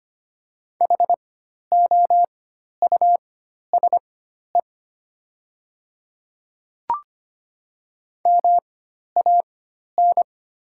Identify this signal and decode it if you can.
Morse code, heard as a keyed tone, about 25 wpm, spaced as for 12 wpm, 690 Hz. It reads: HOUSE MAN
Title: 4x Character Spacing - Top 300 Words - 25wpm